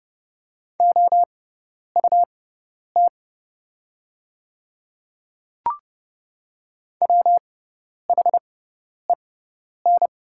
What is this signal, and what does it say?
Morse code, heard as a keyed tone, about 29 wpm, spaced as for 10 wpm, 690 Hz. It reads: OUT WHEN
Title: 6x Character Spacing - Top 200 Words - 30wpm